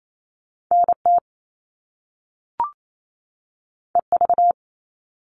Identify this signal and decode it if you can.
Morse code, heard as a keyed tone, about 28 wpm, 700 Hz. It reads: NT EV
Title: All Cut Numbers - 2-Digit 28wpm